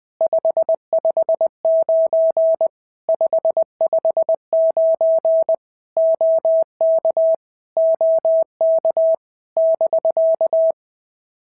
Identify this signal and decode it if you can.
Morse code, heard as a keyed tone, about 20 wpm, 650 Hz. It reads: 559 559 OK OK <BK>
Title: POTA QSO Practice - 20wpm